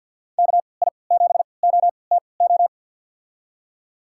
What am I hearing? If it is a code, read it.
Morse code, heard as a keyed tone, about 46 wpm, spaced as for 27 wpm, 700 Hz. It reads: KI7OTO
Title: USA Call Signs - 6 Characters - 3x Character Spacing - 50wpm